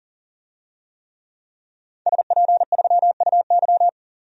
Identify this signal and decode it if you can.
Morse code, heard as a keyed tone, about 40 wpm, 700 Hz. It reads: SP3UY
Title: DXCC-100 Call Signs - 5 Characters - 40wpm